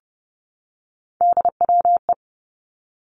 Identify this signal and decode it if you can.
Morse code, heard as a keyed tone, about 30 wpm, 700 Hz. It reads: DWE
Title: Three Challenging Letter-Number Combinations 30wpm